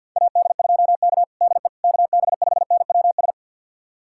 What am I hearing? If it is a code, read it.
Morse code, heard as a keyed tone, about 50 wpm, 700 Hz. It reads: AD2K DE KB5NWS
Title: USA Callsign DE Callsign 50wpm